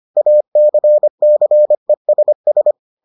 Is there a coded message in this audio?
Yes: Morse code from 0.2 to 2.7 s, heard as a keyed tone, about 25 wpm, 600 Hz. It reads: ACCESS